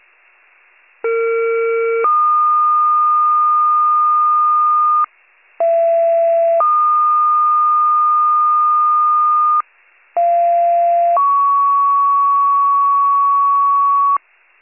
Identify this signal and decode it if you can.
Two-tone page, heard as a one-second tone, then a three-second tone, about 460 then 1220 Hz, about 670 then 1220 Hz, about 690 then 1120 Hz. It is heard from 1.0 to 14.2 s.